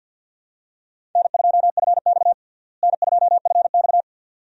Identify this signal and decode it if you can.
Morse code, heard as a keyed tone, about 50 wpm, 700 Hz. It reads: N2FX N2FX
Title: USA Call Signs - 4 Characters - Repeated 2x - 50wpm